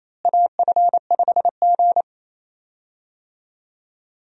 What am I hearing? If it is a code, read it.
Morse code, heard as a keyed tone, about 28 wpm, 700 Hz. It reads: AF5G
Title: USA Call Signs - 4 Characters - Repeated 2x - 28wpm